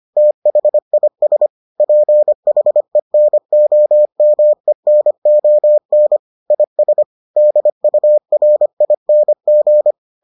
Morse code, heard as a keyed tone, about 25 wpm, 600 Hz. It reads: THIS PHENOMENON IS DURING